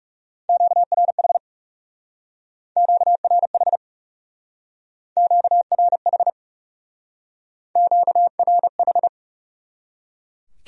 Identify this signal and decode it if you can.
Morse code, heard as a keyed tone, about 36 wpm, 700 Hz. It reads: QRH QRH QRH QRH
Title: Speed-Racing Q-Codes - 30wpm